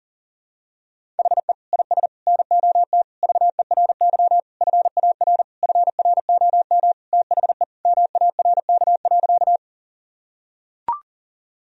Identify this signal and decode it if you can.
Morse code, heard as a keyed tone, about 37 wpm, 700 Hz. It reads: HE IS NOT VERY FAR FROM THE MARK.
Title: Sentences from Top 1000 Words - Review Part 1 - 40wpm